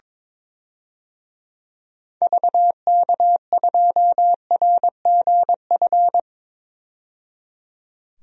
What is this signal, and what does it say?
Morse code, heard as a keyed tone, about 22 wpm, 700 Hz. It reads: VK2RGF
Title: DXCC-100 Call Signs - 6 Characters - 22wpm